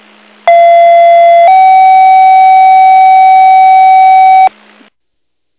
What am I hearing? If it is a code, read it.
Two-tone page, heard as a one-second tone, then a three-second tone, about 680 then 760 Hz, with noise. It is heard from 0.5 to 4.5 s.